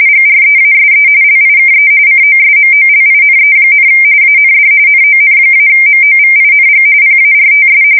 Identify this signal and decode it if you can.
RTTY, about 45.45 baud. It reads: GREASE LIKE COSMOLINE, AND NON-MOVING PARTS
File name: fsk170 / battleshipnj